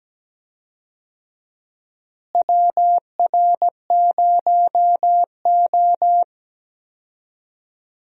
Morse code, heard as a keyed tone, about 17 wpm, 700 Hz. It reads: WR0O
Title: USA Call Signs - 4 Characters - Repeated 3x - 17wpm